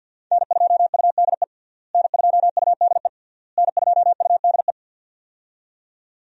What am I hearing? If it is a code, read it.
Morse code, heard as a keyed tone, about 52 wpm, 700 Hz. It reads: N2UDE N2UDE N2UDE
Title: USA Call Signs - 5 Characters - Repeated 3x - 50wpm